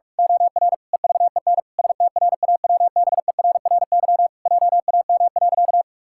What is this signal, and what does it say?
Morse code, heard as a keyed tone, about 46 wpm, 700 Hz. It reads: OR EVEN STRAWBERRY JAM.
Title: Pangram Challenge v2 - 45wpm